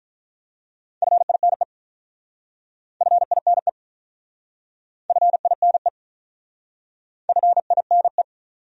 Morse code, heard as a keyed tone, about 47 wpm, 700 Hz. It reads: FINE FINE FINE FINE